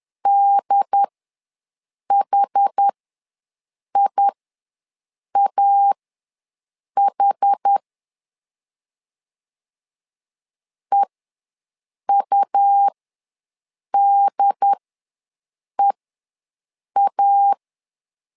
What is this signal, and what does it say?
Morse code, heard as a keyed tone, about 10 wpm, spaced as for 6 wpm, 800 Hz. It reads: DHIAH EUDEA
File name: LEZIONI / 2 UHDN / 2 UHD ws 10wpm 2058